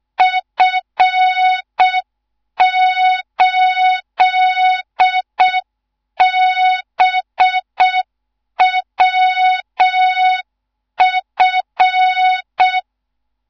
Morse code, heard as a keyed tone, about 6 wpm, 3700 Hz. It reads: F8BWF
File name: Morze-HA2MN / hang / 14 280 HA2MN